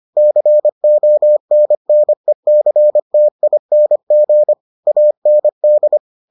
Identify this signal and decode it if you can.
Morse code, heard as a keyed tone, about 25 wpm, 600 Hz. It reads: CONNECTING AND